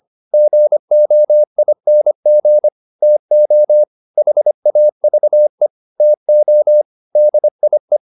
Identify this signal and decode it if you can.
Morse code, heard as a keyed tone, about 25 wpm, 600 Hz. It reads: GOING TO HAVE TO DIE